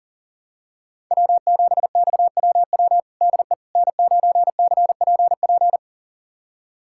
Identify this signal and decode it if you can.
Morse code, heard as a keyed tone, about 40 wpm, 690 Hz. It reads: W7XWW DE N9CPP
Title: USA Callsign DE Callsign - 5 Characters - 40wpm